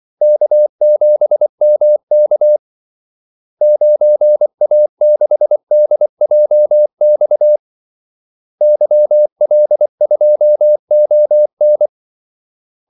Morse code, heard as a keyed tone, about 24 wpm, 600 Hz. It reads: K7MK 9A6DJX YL2ON
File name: Callsigns-24wpm